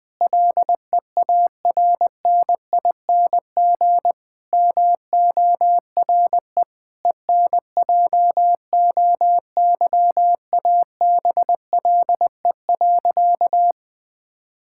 Morse code, heard as a keyed tone, about 20 wpm, 700 Hz. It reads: LEARNING MORE ENJOYABLE.
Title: Difficult Sentences with Hints - 20wpm